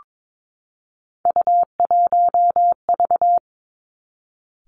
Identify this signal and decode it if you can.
Morse code, heard as a keyed tone, about 22 wpm, 700 Hz. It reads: U1V
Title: Three Challenging Letter-Number Combinations 22wpm